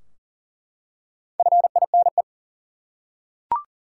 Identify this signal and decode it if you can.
Morse code, heard as a keyed tone, about 40 wpm, 700 Hz. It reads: FINE
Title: Four Letter Words - 40wpm